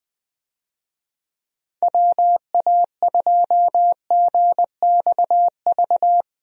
Morse code, heard as a keyed tone, about 21 wpm, 700 Hz. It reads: WA2GXV
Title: USA Call Signs - 6 Characters - Repeated 2x - 20wpm